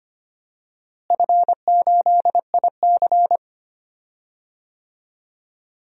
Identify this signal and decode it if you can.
Morse code, heard as a keyed tone, about 25 wpm, 700 Hz. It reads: F8IC
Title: DXCC-100 Call Signs - 4 Characters - Repeated 2x - 25wpm